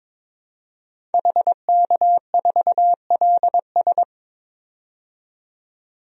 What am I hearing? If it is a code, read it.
Morse code, heard as a keyed tone, about 22 wpm, 700 Hz. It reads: HK4LS
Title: DXCC-100 Call Signs - 5 Characters - 22wpm